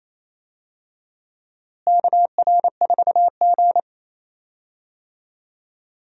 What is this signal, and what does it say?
Morse code, heard as a keyed tone, about 28 wpm, 700 Hz. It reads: KR4G
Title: USA Call Signs - 4 Characters - Repeated 3x - 28wpm